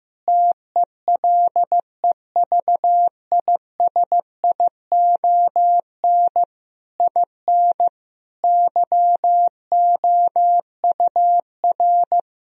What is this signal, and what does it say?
Morse code, heard as a keyed tone, about 15 wpm, 700 Hz. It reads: TELEVISION IN YOUR
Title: Sentences from Top 1000 Words - Review Part 4 - 15wpm